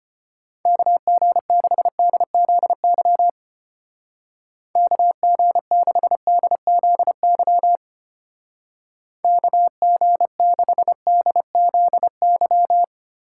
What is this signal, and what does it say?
Morse code, heard as a keyed tone, about 31 wpm, 700 Hz. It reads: KG6DZY KG6DZY KG6DZY